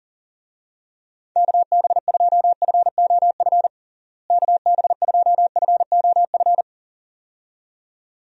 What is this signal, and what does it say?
Morse code, heard as a keyed tone, about 40 wpm, 700 Hz. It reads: KB2FOF KB2FOF
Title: USA Call Signs - 6 Characters - Repeated 2x - 40wpm